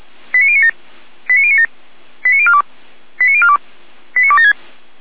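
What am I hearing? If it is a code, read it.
Five-tone selective call (ZVEI), heard as quick runs of five tones, tones of about 70 ms. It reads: 89097 89097 89042 89042 89276
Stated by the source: ZVEI five-tone call received on SDR.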